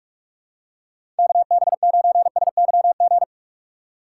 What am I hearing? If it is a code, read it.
Morse code, heard as a keyed tone, about 44 wpm, 700 Hz. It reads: KB9SYG
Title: USA Call Signs - 6 Characters - 2x Character Spacing - 45wpm